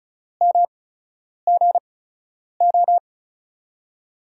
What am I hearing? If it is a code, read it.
Morse code, heard as a keyed tone, about 36 wpm, spaced as for 10 wpm, 700 Hz. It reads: MGO
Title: USA Call Signs - 5 Characters - 8x Character Spacing - 35wpm